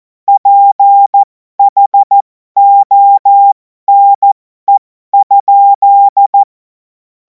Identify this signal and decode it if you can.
Morse code, heard as a keyed tone, about 14 wpm, spaced as for 12 wpm, 800 Hz. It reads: PHONE?
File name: Morsekurs / Witze